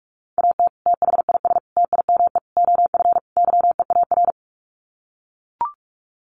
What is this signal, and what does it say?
Morse code, heard as a keyed tone, about 44 wpm, 690 Hz. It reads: AT THIS TIME OF YEAR
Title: Sets of 5 Words Encore2 45wpm